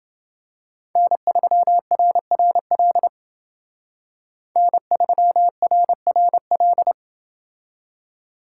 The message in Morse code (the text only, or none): N3RRL N3RRL